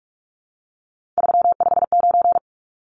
Morse code, heard as a keyed tone, about 44 wpm, 700 Hz. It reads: 359
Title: Numbers - 3-Digit 45wpm